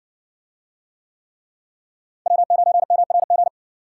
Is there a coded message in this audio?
Yes: Morse code from 2.3 to 3.5 s, heard as a keyed tone, about 58 wpm, 700 Hz. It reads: W9MRG